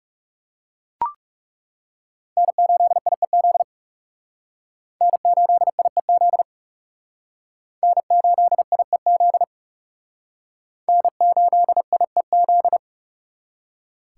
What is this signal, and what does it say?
Morse code, heard as a keyed tone, about 37 wpm, 700 Hz. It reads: N8IEZ N8IEZ N8IEZ N8IEZ
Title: Speed Racing - USA Call Signs 30wpm